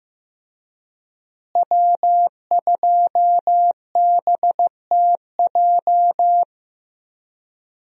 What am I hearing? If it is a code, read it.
Morse code, heard as a keyed tone, about 15 wpm, 700 Hz. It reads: W2BTJ
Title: USA Call Signs - 5 Characters - Repeated 2x - 15wpm